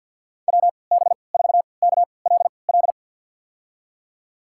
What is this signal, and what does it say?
Morse code, heard as a keyed tone, about 47 wpm, spaced as for 28 wpm, 700 Hz. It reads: WB4KLL
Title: USA Call Signs - 6 Characters - 3x Character Spacing - 50wpm